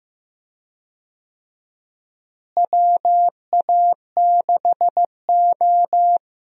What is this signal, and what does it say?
Morse code, heard as a keyed tone, about 16 wpm, 700 Hz. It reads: WA6O
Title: USA Call Signs - 4 Characters - Repeated 2x - 15wpm